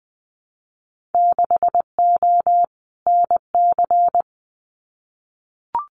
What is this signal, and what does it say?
Morse code, heard as a keyed tone, about 20 wpm, 700 Hz. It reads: 6O NC
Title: Winter Field Day Exchanges 20wpm